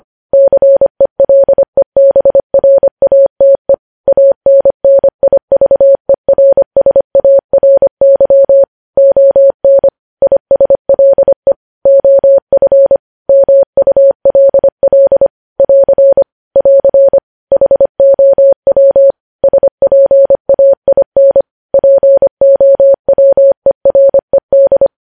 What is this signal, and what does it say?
Morse code, heard as a keyed tone, about 25 wpm, 550 Hz. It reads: CELEBRATE ANNIVERSARY ON ISLE OF MULL <AR> <AR> HOW SPAIN POWERED
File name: podcast / News headlines-1x-25x25